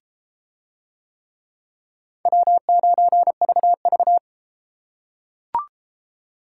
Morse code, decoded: W9VV

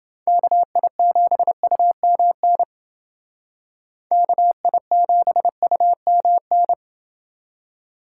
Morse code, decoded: KI7UMN KI7UMN

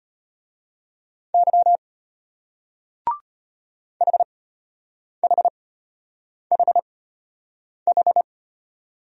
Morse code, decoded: Y HHHH

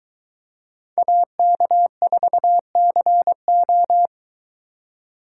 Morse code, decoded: AK4CO

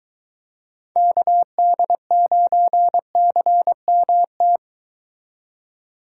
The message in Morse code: KD9CMT

